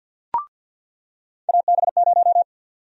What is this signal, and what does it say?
Morse code, heard as a keyed tone, about 53 wpm, 700 Hz. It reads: AB0
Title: USA Call Sign Prefixes - 50wpm